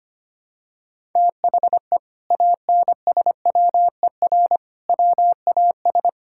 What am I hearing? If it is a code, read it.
Morse code, heard as a keyed tone, about 25 wpm, 700 Hz. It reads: THE ANSWER WAS
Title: Sentences from Top 1000 Words - Review Part 6 - 25wpm